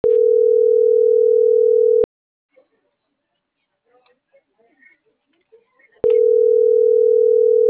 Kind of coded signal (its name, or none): telephone call progress tone